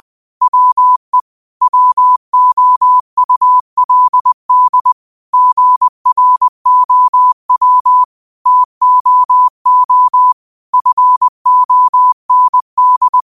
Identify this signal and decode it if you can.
Morse code, heard as a keyed tone, about 20 wpm, 1000 Hz. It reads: WE WOULD GROW TOO FOND
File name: qotdfiles / 20wpm